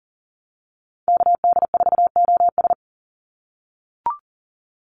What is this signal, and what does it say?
Morse code, heard as a keyed tone, about 38 wpm, 700 Hz. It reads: KD4OS
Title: Call Signs - 40wpm